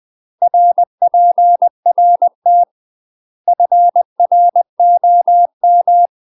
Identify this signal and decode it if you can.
Morse code, heard as a keyed tone, about 20 wpm, 700 Hz. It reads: RPRT FROM
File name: UK MASTER QSOs TRAINING / 20WPM / UK MASTER QSOs 0007